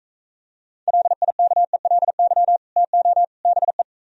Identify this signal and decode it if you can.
Morse code, heard as a keyed tone, about 45 wpm, 700 Hz. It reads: LIKELY TO BE